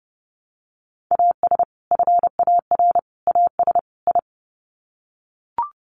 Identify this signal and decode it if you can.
Morse code, heard as a keyed tone, about 30 wpm, 700 Hz. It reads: AS FAR AS I
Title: Sets of 4 Words 30wpm